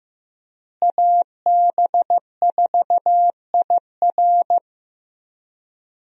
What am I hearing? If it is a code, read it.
Morse code, heard as a keyed tone, about 15 wpm, 700 Hz. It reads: AB4IR
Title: USA Call Signs - 5 Characters - Repeated 2x - 15wpm